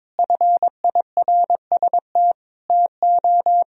Morse code, heard as a keyed tone, about 22 wpm, 700 Hz. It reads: FIRST TO
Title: Sentences from Top 1000 Words - Review Part 1 - 22wpm